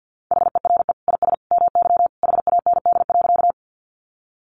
Morse code, heard as a keyed tone, about 50 wpm, 700 Hz. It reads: HERE IS MY HAND.
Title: Sentences from Top 200 Words 50wpm